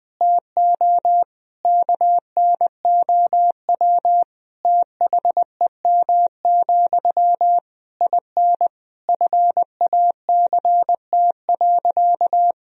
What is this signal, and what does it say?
Morse code, heard as a keyed tone, about 20 wpm, 700 Hz. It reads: TO KNOW THEM, IN FACT.
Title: Sentences from Top 1000 Words - Review Part 3 - 20wpm